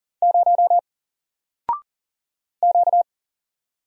Morse code, heard as a keyed tone, about 40 wpm, 700 Hz. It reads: 0 Q
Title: Single Letter-Number Alt 40wpm